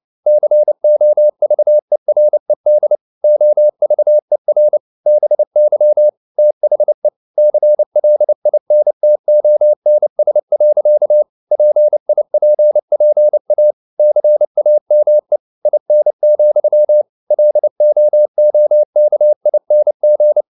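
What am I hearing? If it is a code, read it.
Morse code, heard as a keyed tone, about 29 wpm, 600 Hz. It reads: COVERED OVER BY THE CLINTONS. PIPPA CAME IN, LOOKING
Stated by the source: Morse code at 29 wpm, sending Ken Follett's novel Never in English, from CW med Gnister.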